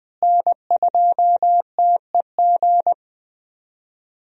Morse code, heard as a keyed tone, about 20 wpm, 700 Hz. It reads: N2TEG
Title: USA Call Signs - 5 Characters - Repeated 2x - 20wpm